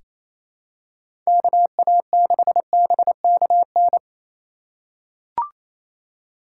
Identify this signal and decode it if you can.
Morse code, heard as a keyed tone, about 27 wpm, 700 Hz. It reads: KA6BKN
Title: USA Call Signs - 2x Character Spacing - 28wpm